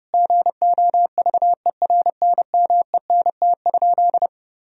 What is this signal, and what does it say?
Morse code, heard as a keyed tone, about 30 wpm, 700 Hz. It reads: GOVERNMENT?
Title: Sentences from Top 1000 Words - Review Part 3 - 30wpm